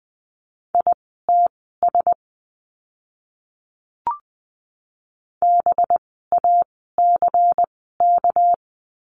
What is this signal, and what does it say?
Morse code, heard as a keyed tone, about 20 wpm, spaced as for 15 wpm, 700 Hz. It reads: ITS BACK
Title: 206 Lesson 33f - Letter j - Words 20wpm